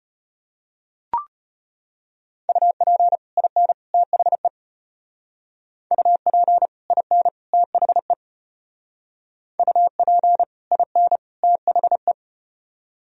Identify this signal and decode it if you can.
Morse code, heard as a keyed tone, about 34 wpm, 700 Hz. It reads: UP IN THE UP IN THE UP IN THE